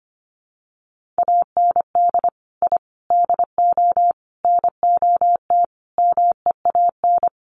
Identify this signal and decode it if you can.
Morse code, heard as a keyed tone, about 25 wpm, 700 Hz. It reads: AND I DO NOT MEAN
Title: Sets of 5 Words Encore2 25wpm